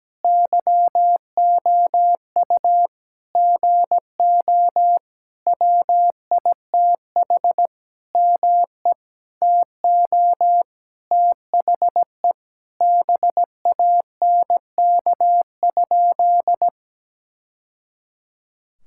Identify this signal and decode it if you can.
Morse code, heard as a keyed tone, about 17 wpm, 700 Hz. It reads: YOU GO WITH ME TO THE BANK?
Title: Sentences from Top 1000 Words - Review Part 6 - 17wpm